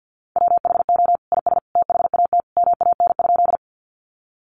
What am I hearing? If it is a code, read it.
Morse code, heard as a keyed tone, about 50 wpm, 700 Hz. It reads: WHO IS THAT MAN?
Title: Sentences from Top 200 Words 50wpm